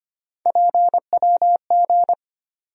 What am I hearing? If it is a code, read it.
Morse code, heard as a keyed tone, about 25 wpm, 700 Hz. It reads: PWG
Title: USA Call Sign Postfixes - 3 Letters 25wpm